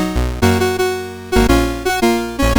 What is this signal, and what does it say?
There is no code.